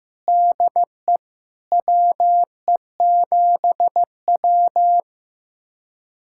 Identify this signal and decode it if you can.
Morse code, heard as a keyed tone, about 15 wpm, 700 Hz. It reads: DE WE7W
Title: USA Callsign DE Callsign - 4 Characters - 15wpm